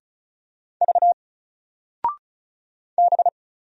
Morse code, heard as a keyed tone, about 36 wpm, 700 Hz. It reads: V B
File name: Single-Letter-Number-35wpm